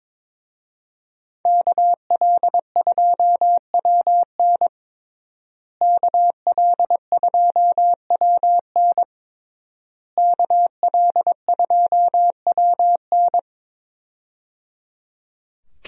Morse code, heard as a keyed tone, about 22 wpm, 700 Hz. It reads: KL2WN KL2WN KL2WN
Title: USA Call Signs - 5 Characters - Repeated 3x - 22wpm